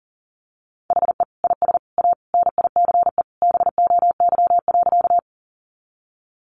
Morse code, heard as a keyed tone, about 40 wpm, 700 Hz. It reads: HE IS A NICE BOY.
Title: Sentences from Top 1100 Words 40wpm